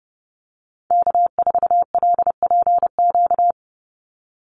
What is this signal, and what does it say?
Morse code, heard as a keyed tone, about 30 wpm, 700 Hz. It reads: K4LPQ
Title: NA SOTA Activator Callsigns 30wpm